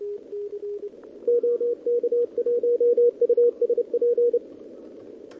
Morse code, heard as a keyed tone, about 30 wpm, 480 Hz, with radio noise. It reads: OK1USP
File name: OK QRP 2026 / OKQRP2026